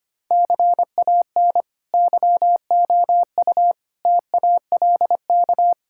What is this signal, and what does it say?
Morse code, heard as a keyed tone, about 25 wpm, 700 Hz. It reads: CAN YOU TALK
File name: practice / 25wpm / Sentences with 2-Word Phrases - 25wpm